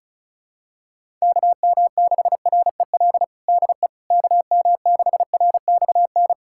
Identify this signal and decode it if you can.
Morse code, heard as a keyed tone, about 36 wpm, 700 Hz. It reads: KM6REL DE KM6RXN
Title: USA Callsign DE Callsign - 6 Characters - 35wpm